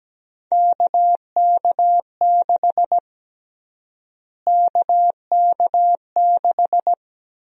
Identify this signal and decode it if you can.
Morse code, heard as a keyed tone, about 17 wpm, 700 Hz. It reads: KK6 KK6
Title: USA Call Sign Prefixes - Repeated 2x - 17wpm